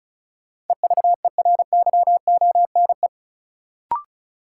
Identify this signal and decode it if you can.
Morse code, heard as a keyed tone, about 36 wpm, 700 Hz. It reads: EVERYONE
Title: Sentences from Top 800 Words - New Word Drill - 35wpm